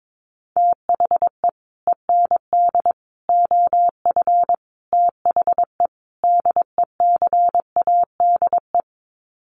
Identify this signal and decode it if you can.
Morse code, heard as a keyed tone, about 22 wpm, 700 Hz. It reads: THE END OF THE DECADE
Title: Sets of 5 Words Encore 22wpm